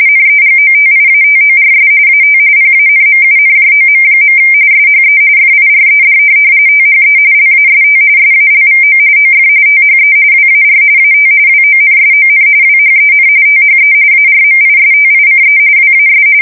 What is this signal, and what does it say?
RTTY, about 45.45 baud. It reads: NEGOTIATORS AT PANMUNJOM. SHE FIRED THE LAST SALVO OF THE WAR AT WONSON WHEN THE ARMISTICE WAS
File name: fsk170 / USS-New-Jersey